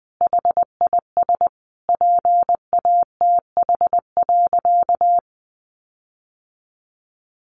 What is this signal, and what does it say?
Morse code, heard as a keyed tone, about 20 wpm, 700 Hz. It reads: HIS PATH.
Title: Sentences from Top 1400 Words 20wpm